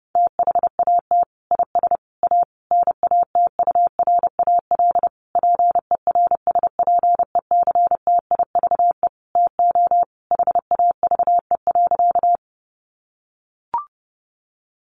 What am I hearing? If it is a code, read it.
Morse code, heard as a keyed tone, about 30 wpm, 700 Hz. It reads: THAT IS A NATURAL PERSPECTIVE TO HAVE.
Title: Sentences from Top 1200 Words 30wpm